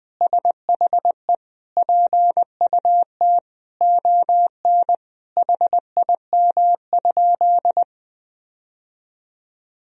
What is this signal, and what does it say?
Morse code, heard as a keyed tone, about 20 wpm, 700 Hz. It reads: SHE PUT ON HIM?
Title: Sentences from Top 1000 Words - Review Part 1 - 20wpm